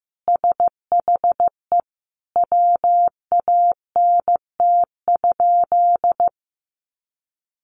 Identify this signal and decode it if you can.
Morse code, heard as a keyed tone, about 15 wpm, 700 Hz. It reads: SHE WANT?
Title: Sentences from Top 200 Words 15wpm